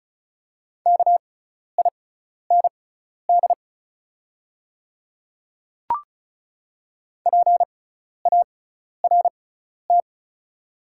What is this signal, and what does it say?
Morse code, heard as a keyed tone, about 36 wpm, spaced as for 12 wpm, 690 Hz. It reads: KIND PART
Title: 6x Character Spacing - Top 200 Words - 35wpm